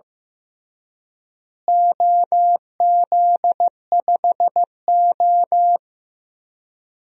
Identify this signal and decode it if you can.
Morse code, heard as a keyed tone, about 16 wpm, 700 Hz. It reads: OZ5O
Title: DXCC-100 Call Signs - 4 Characters - Repeated 3x - 15wpm